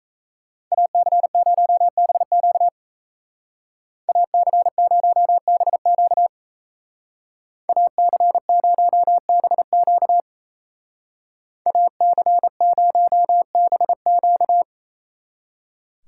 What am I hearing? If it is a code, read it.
Morse code, heard as a keyed tone, about 34 wpm, 700 Hz. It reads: AC0BQ AC0BQ AC0BQ AC0BQ
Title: Speed Racing - USA Call Signs 28wpm